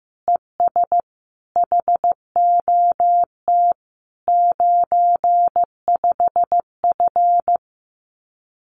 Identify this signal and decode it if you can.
Morse code, heard as a keyed tone, about 15 wpm, 700 Hz. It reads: ES HOT 95F
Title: Common QSO Phrases Encore 15wpm